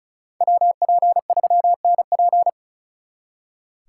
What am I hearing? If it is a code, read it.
Morse code, heard as a keyed tone, about 46 wpm, 700 Hz. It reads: WP3NP